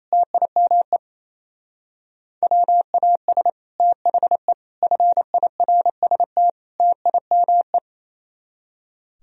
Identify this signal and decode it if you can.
Morse code, heard as a keyed tone, about 35 wpm, 700 Hz. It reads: TIME WAS THE FIRST TIME